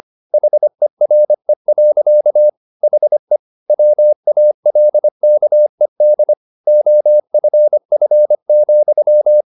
Morse code, heard as a keyed tone, about 25 wpm, 600 Hz. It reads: HERE. HE WALKED OFF,
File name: the old man and the sea part 1 25wpm